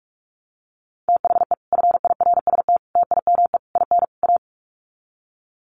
Morse code, heard as a keyed tone, about 44 wpm, 700 Hz. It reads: THE FIRST TIME IN A
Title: Sets of 5 Words Encore 45wpm